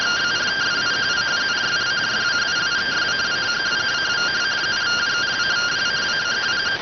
Two-tone teleprinter signal, about 170 Hz shift.